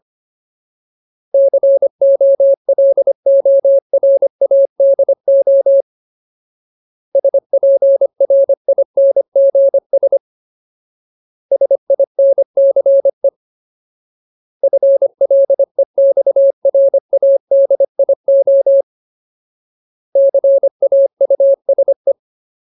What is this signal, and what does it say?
Morse code, heard as a keyed tone, about 25 wpm, 550 Hz. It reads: COLORADO SPRINGS SINCE FLEXRADIO CAUSE
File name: Ultimate Word List / 25WPM Ultimate